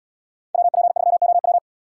Morse code, heard as a keyed tone, about 71 wpm, 700 Hz. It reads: WP3OW